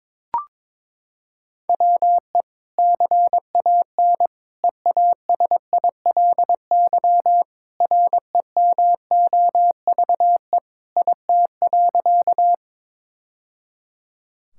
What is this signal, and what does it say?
Morse code, heard as a keyed tone, about 23 wpm, 700 Hz. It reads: WE CAN EASILY REMOVE IT.
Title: Sentences from Top 1000 Words - Review Part 5 - 22wpm